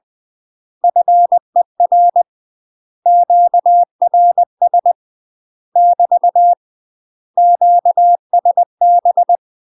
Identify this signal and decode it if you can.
Morse code, heard as a keyed tone, about 20 wpm, 700 Hz. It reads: FER QRS <BT> QSB
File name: UK MASTER QSOs TRAINING / 20WPM / UK MASTER QSOs 0020